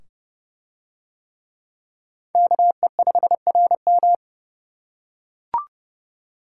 Morse code, heard as a keyed tone, about 26 wpm, 700 Hz. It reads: KE5RM